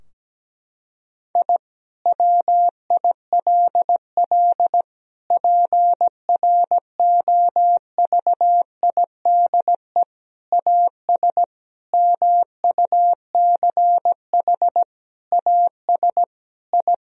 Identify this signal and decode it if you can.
Morse code, heard as a keyed tone, about 17 wpm, 700 Hz. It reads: I WILL PROVIDE AS MUCH AS I
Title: Sentences from Top 1000 Words - Review Part 3 - 17wpm